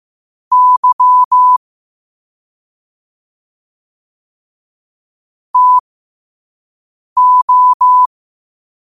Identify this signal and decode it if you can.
Morse code, heard as a keyed tone, about 15 wpm, spaced as for 5 wpm, 1000 Hz. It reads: Y TO
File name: qotdfiles / 5wpm